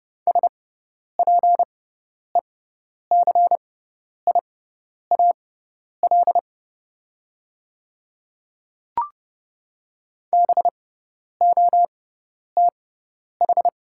Morse code, heard as a keyed tone, about 30 wpm, spaced as for 10 wpm, 700 Hz. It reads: SPECIAL BOTH
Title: 6x Character Spacing - Top 500 Words - 30wpm